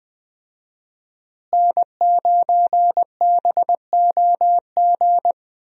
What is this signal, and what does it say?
Morse code, heard as a keyed tone, about 20 wpm, 700 Hz. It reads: N9BOG
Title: USA Call Signs - 5 Characters - Repeated 3x - 20wpm